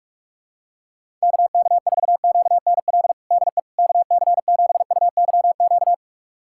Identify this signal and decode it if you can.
Morse code, heard as a keyed tone, about 46 wpm, 700 Hz. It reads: KK4QNL DE KC7UYQ
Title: USA Callsign DE Callsign - 6 Characters - 45wpm